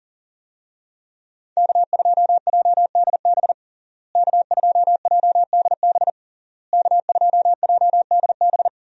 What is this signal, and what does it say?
Morse code, heard as a keyed tone, about 40 wpm, 700 Hz. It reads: K2JDB K2JDB K2JDB
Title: USA Call Signs - 5 Characters - Repeated 3x - 40wpm